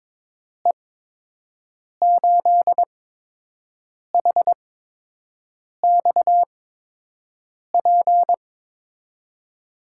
Morse code, heard as a keyed tone, about 22 wpm, spaced as for 6 wpm, 700 Hz. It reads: E8HXP
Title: USA Call Signs - 6 Characters - 8x Character Spacing - 22wpm